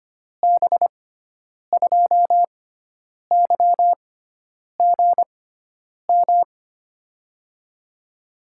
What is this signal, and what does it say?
Morse code, heard as a keyed tone, about 25 wpm, spaced as for 9 wpm, 700 Hz. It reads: B2YGM
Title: USA Call Signs - 6 Characters - 6x Character Spacing - 25wpm